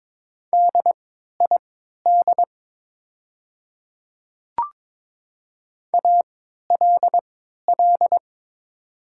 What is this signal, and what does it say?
Morse code, heard as a keyed tone, about 22 wpm, spaced as for 13 wpm, 700 Hz. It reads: DID ALL
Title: 3x Character Spacing - Top 100 Words - 22wpm